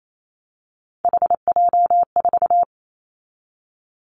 Morse code, heard as a keyed tone, about 28 wpm, 700 Hz. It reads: HJ4
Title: Callsign Prefixes for Top 100 DXCC - 28wpm